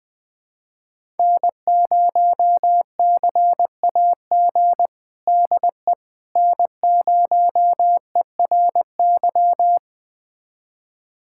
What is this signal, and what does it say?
Morse code, heard as a keyed tone, about 20 wpm, 700 Hz. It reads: N0CAG DE N0ERY
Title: USA Callsign DE Callsign - 5 Characters - 20wpm